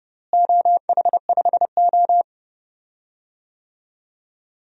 Morse code, heard as a keyed tone, about 30 wpm, 700 Hz. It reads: OH5O
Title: DXCC-100 Call Signs - 4 Characters - Repeated 3x - 30wpm